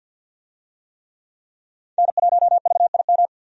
Morse code, heard as a keyed tone, about 51 wpm, 700 Hz. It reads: N1VIM
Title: USA Call Signs - 5 Characters - 50wpm